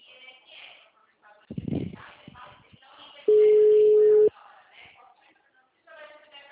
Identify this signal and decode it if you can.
Ringback tone (Europe, 425 Hz), with speech.
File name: out- -2008- - - .10093